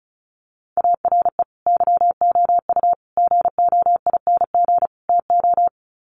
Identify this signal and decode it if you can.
Morse code, heard as a keyed tone, about 34 wpm, 700 Hz. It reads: ARE YOU GOING TO
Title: Sets of 4 Words 35wpm